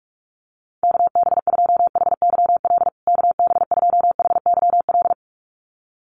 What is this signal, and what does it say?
Morse code, heard as a keyed tone, about 44 wpm, 700 Hz. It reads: KB2HYL KB2HYL
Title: USA Call Signs Repeated 2x 45wpm